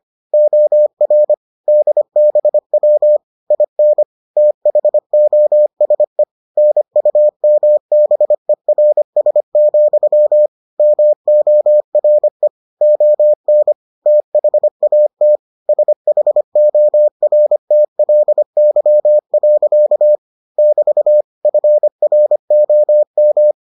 Morse code, heard as a keyed tone, about 25 wpm, 600 Hz. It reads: OR DBW IN THOSE NUMBERS, MORE ON THAT SHORTLY. <BT> FROM